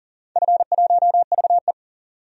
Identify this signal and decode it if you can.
Morse code, heard as a keyed tone, about 40 wpm, 700 Hz. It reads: F1VE
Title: DXCC-100 Call Signs - 4 Characters - 40wpm